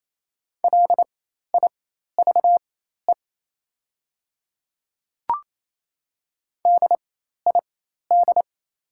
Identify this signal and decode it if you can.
Morse code, heard as a keyed tone, about 28 wpm, spaced as for 13 wpm, 700 Hz. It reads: LIVE DID
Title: 4x Character Spacing - Top 500 Words - 28wpm